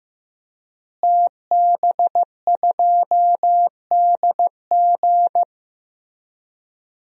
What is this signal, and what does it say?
Morse code, heard as a keyed tone, about 15 wpm, 700 Hz. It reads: TB2DG
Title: DXCC-100 Call Signs - 5 Characters - Repeated 2x - 15wpm